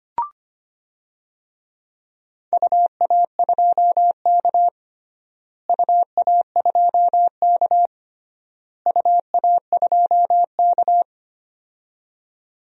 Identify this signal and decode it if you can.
Morse code, heard as a keyed tone, about 25 wpm, 700 Hz. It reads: UA2K UA2K UA2K
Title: DXCC-100 Call Signs - 4 Characters - Repeated 3x - 25wpm